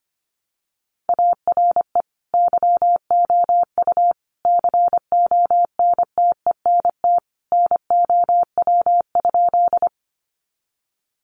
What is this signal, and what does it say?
Morse code, heard as a keyed tone, about 25 wpm, 700 Hz. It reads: ARE YOU CONTENT NOW?
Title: Sentences from Top 900 Words 25wpm